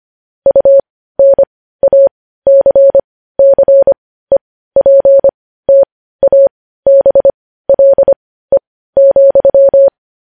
Morse code, heard as a keyed tone, about 25 wpm, spaced as for 15 wpm, 550 Hz. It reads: UNACCEPTABLE,
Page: news-headlines-in-morse-code-at-15-wpm